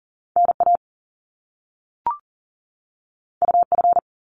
Morse code, heard as a keyed tone, about 40 wpm, 700 Hz. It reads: NA UF
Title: Top Two Letter Combinations in Words 40wpm